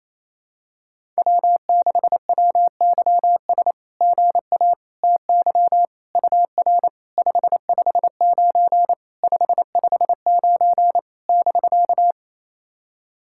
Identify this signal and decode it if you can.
Morse code, heard as a keyed tone, about 29 wpm, 700 Hz. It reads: W6WYS GA TY UR 559 559 <BK>
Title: SOTA QSO Practice 28wpm